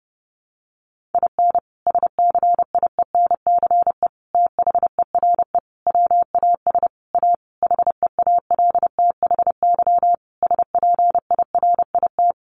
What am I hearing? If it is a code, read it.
Morse code, heard as a keyed tone, about 30 wpm, 700 Hz. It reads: IN SCIENCE THERE WAS A HEALTHY SPIRIT
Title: Sentences from Top 1400 Words 30wpm